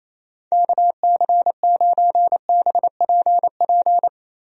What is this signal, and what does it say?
Morse code, heard as a keyed tone, about 28 wpm, 700 Hz. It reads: KC9BPP
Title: USA Call Signs - 6 Characters - 28wpm